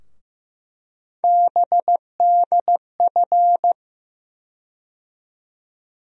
Morse code, heard as a keyed tone, about 15 wpm, 700 Hz. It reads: BDF
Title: USA Call Sign Postfixes - 3 Letters 15wpm